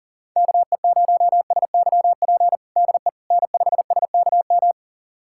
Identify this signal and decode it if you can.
Morse code, heard as a keyed tone, about 40 wpm, 700 Hz. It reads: KE0SYP DE N5SKM
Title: USA Callsign DE Callsign 40wpm